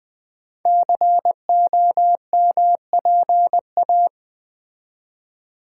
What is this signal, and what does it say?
Morse code, heard as a keyed tone, about 20 wpm, 700 Hz. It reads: COMPA
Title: Build A Word - 5 Letters At A Time - 10-Letter Words - 20wpm